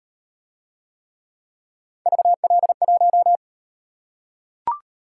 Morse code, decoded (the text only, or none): VL1